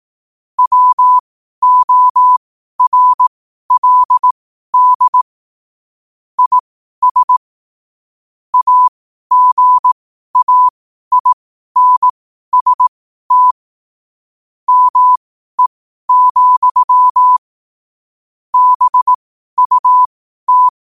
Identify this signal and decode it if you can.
Morse code, heard as a keyed tone, about 18 wpm, spaced as for 12 wpm, 1000 Hz. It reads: WORLD IS AGAINST ME, BUT
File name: qotdfiles / 12wpm